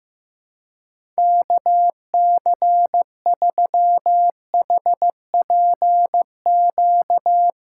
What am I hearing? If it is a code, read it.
Morse code, heard as a keyed tone, about 15 wpm, 700 Hz. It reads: KC3HPQ
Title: USA Call Signs - 6 Characters - 3x Character Spacing - 15wpm